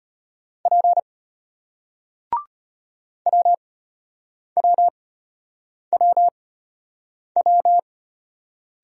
Morse code, heard as a keyed tone, about 33 wpm, spaced as for 8 wpm, 700 Hz. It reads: P WWWW